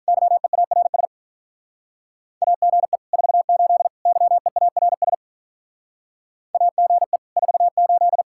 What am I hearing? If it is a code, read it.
Morse code, heard as a keyed tone, about 46 wpm, 700 Hz. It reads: YEARS AGE 48 YEARS AGE 48